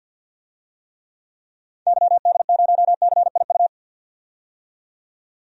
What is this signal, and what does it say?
Morse code, heard as a keyed tone, about 50 wpm, 700 Hz. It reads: YD0CIU
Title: DXCC-100 Call Signs - 6 Characters - 50wpm